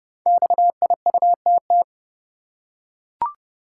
Morse code, heard as a keyed tone, about 29 wpm, 700 Hz. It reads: XIUTT
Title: Code Groups - 5 Letters - 30wpm